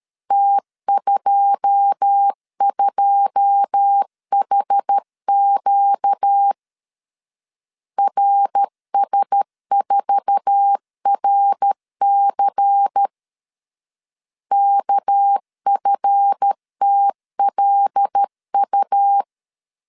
Morse code, heard as a keyed tone, about 13 wpm, 800 Hz. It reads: T22HQ RS4RC KFTLU
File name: LEZIONI / 9 2 12wpm 2301